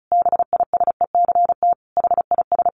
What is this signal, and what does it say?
Morse code, heard as a keyed tone, about 34 wpm, 700 Hz. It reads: BISECT HIS
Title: A Sound Of Thunder 35wpm